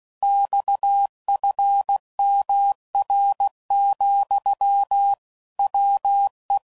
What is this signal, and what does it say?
Morse code, heard as a keyed tone, about 16 wpm, 800 Hz. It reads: XFMR, WE
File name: CW Code Groups @15WPM/Technical Subjects